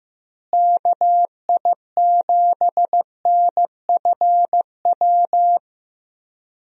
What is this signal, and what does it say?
Morse code, heard as a keyed tone, about 15 wpm, 700 Hz. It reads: KI7NFW
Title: USA Call Signs - 6 Characters - Repeated 2x - 15wpm